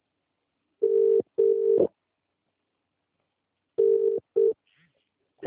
Ringback tone (UK), 400 + 450 Hz.